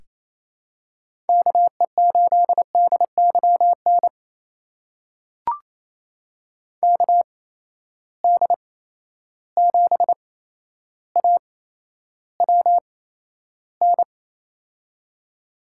Morse code, heard as a keyed tone, about 28 wpm, spaced as for 12 wpm, 700 Hz. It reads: KE8DYN KD7AWN
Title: USA Call Signs - 6 Characters - 8x Character Spacing - 28wpm